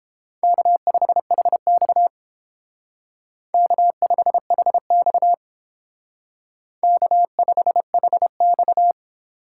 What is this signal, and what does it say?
Morse code, heard as a keyed tone, about 28 wpm, 700 Hz. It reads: K5HX K5HX K5HX